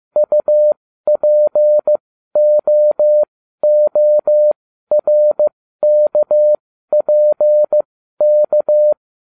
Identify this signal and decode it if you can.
Morse code, heard as a keyed tone, about 15 wpm, spaced as for 12 wpm, 600 Hz. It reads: UPOORKPK
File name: cw / ubacw09 1